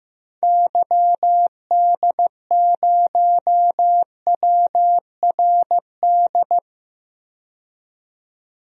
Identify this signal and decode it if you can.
Morse code, heard as a keyed tone, about 15 wpm, 700 Hz. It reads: YD0WRD
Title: DXCC-100 Call Signs - 6 Characters - 15wpm